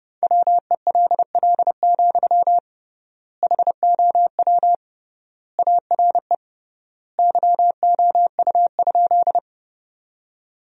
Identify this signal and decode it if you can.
Morse code, heard as a keyed tone, about 30 wpm, 700 Hz. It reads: WELL, HOW ARE YOU?
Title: 3x Word-Spacing - Sentences from Top 200 Words - 30wpm